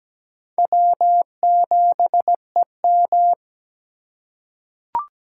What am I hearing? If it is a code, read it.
Morse code, heard as a keyed tone, about 17 wpm, 700 Hz. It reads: W7EM
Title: USA Call Signs - 4 Characters - 3x Character Spacing - 17wpm